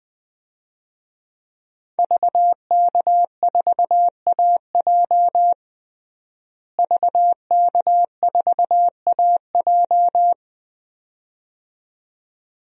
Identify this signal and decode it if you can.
Morse code, heard as a keyed tone, about 21 wpm, 700 Hz. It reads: VK4AJ VK4AJ
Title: DXCC-100 Call Signs - 5 Characters - Repeated 2x - 20wpm